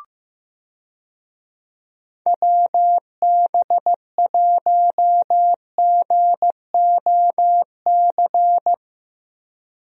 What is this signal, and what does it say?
Morse code, heard as a keyed tone, about 15 wpm, 700 Hz. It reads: WB1GOC
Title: USA Call Signs - 6 Characters - Repeated 3x - 15wpm